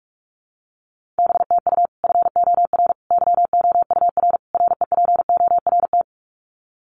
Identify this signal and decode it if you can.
Morse code, heard as a keyed tone, about 44 wpm, 700 Hz. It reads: BTU FOR YOUR REPORT
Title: Common QSO Phrases 45wpm